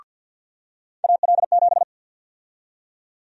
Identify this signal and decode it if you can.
Morse code, heard as a keyed tone, about 52 wpm, 700 Hz. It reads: AL7
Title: USA Call Sign Prefixes - 50wpm